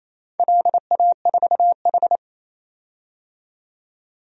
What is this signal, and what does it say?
Morse code, heard as a keyed tone, about 28 wpm, 700 Hz. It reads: LA4H
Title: DXCC-100 Call Signs - 4 Characters - 28wpm